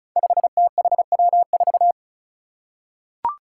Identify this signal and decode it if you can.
Morse code, heard as a keyed tone, about 34 wpm, 700 Hz. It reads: 5THW4
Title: Five Challenging Letter-Number Combinations 35wpm